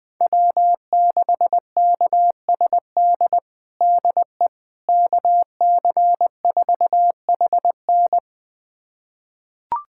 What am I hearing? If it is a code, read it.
Morse code, heard as a keyed tone, about 20 wpm, 700 Hz. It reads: W6KSD DE KC4HN
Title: USA Callsign DE Callsign - 5 Characters - 20wpm